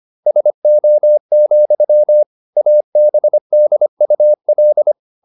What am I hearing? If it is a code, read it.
Morse code, heard as a keyed tone, about 25 wpm, 600 Hz. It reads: SO, ABDUL
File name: Never 0025wpm